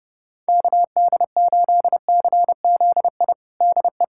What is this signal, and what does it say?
Morse code, heard as a keyed tone, about 30 wpm, 700 Hz. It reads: KD8CZI DE
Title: USA Callsign DE Callsign 30wpm